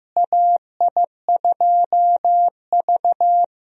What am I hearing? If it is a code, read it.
Morse code, heard as a keyed tone, about 15 wpm, 700 Hz. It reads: AI2V
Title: USA Call Signs - 4 Characters - Repeated 3x - 15wpm